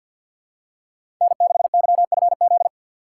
Morse code, heard as a keyed tone, about 46 wpm, 700 Hz. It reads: N6YFZ
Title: USA Call Signs - 3x Character Spacing - 50wpm